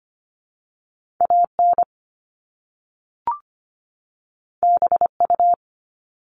Morse code, heard as a keyed tone, about 25 wpm, 700 Hz. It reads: AN BU
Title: Top Two Letter Combinations in Words 25wpm